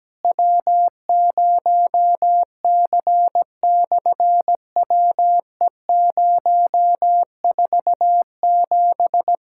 Morse code, heard as a keyed tone, about 17 wpm, 700 Hz. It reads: W0C/WE047
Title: SOTA QSO Practice 17wpm